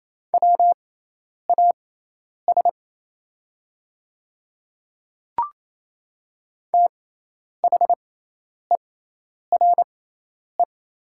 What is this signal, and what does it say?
Morse code, heard as a keyed tone, about 27 wpm, spaced as for 10 wpm, 690 Hz. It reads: WAS THERE
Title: 6x Character Spacing - Top 100 Words - 28wpm